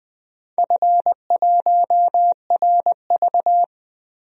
Morse code, heard as a keyed tone, about 21 wpm, 700 Hz. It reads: F1RV
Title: DXCC-100 Call Signs - 4 Characters - Repeated 3x - 20wpm